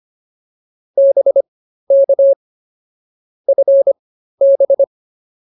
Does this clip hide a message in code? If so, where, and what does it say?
Yes: Morse code from 1.0 to 4.8 s, heard as a keyed tone, about 24 wpm, spaced as for 13 wpm, 550 Hz. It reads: BK FB